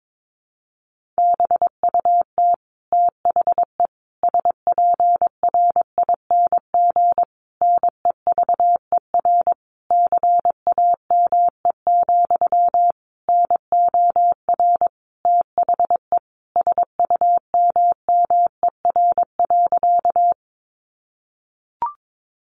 Morse code, heard as a keyed tone, about 22 wpm, 700 Hz. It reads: BUT THE SPRING NEVER CAME, NOR THE SUMMER.
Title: Sentences from Top 1500 Words 22wpm